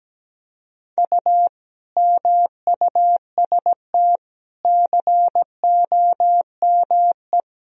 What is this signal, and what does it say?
Morse code, heard as a keyed tone, about 17 wpm, 700 Hz. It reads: U MUST COME
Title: Sentences from Top 200 Words and Abbreviations 17wpm